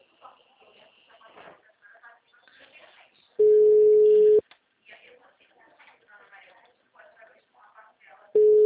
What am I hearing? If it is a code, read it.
Ringback tone (Europe, 425 Hz), with speech.